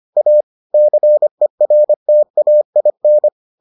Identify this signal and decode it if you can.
Morse code, heard as a keyed tone, about 25 wpm, 600 Hz. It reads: A CERTAIN